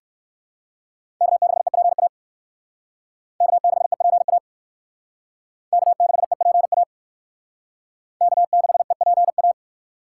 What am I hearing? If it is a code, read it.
Morse code, heard as a keyed tone, about 56 wpm, 700 Hz. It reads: K6EPA K6EPA K6EPA K6EPA